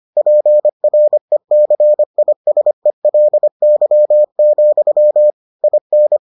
Morse code, heard as a keyed tone, about 25 wpm, 600 Hz. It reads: PRECISELY, IN